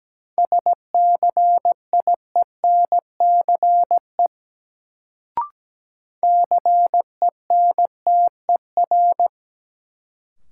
Morse code, heard as a keyed tone, about 17 wpm, 700 Hz. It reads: SCIENCE CENTER
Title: Sentences from Top 1000 Words - Review Part 2 - New Word Drill - 17wpm